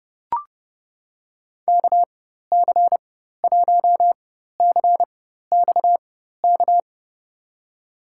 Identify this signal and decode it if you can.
Morse code, heard as a keyed tone, about 30 wpm, spaced as for 14 wpm, 700 Hz. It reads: KC1CXK
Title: USA Call Signs - 6 Characters - 4x Character Spacing - 30wpm